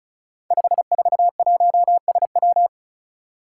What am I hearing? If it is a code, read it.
Morse code, heard as a keyed tone, about 36 wpm, 700 Hz. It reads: 541SW
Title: Five Challenging Letter-Number Combinations 35wpm